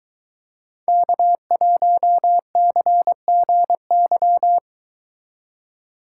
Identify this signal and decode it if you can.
Morse code, heard as a keyed tone, about 22 wpm, 700 Hz. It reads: K1CGY